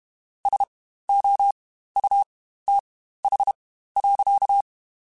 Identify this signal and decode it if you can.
Morse code, heard as a keyed tone, about 32 wpm, spaced as for 15 wpm, 800 Hz. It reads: SOUTH.
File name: Spaced Characters / Spaced Characters-mp3s / Weather 4 21 sp @30